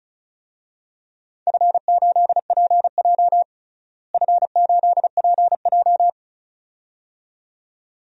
Morse code, heard as a keyed tone, about 36 wpm, 700 Hz. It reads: F8PJ F8PJ
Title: DXCC-100 Call Signs - 4 Characters - Repeated 2x - 35wpm